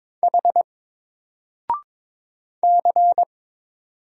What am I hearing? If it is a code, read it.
Morse code, heard as a keyed tone, about 22 wpm, 700 Hz. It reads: H C
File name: Single-Letter-Number-22wpm